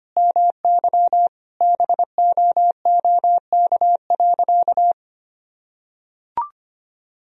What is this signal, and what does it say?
Morse code, heard as a keyed tone, about 25 wpm, 700 Hz. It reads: MY BOOK.
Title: Sentences from Top 1000 Words - Review Part 3 - 25wpm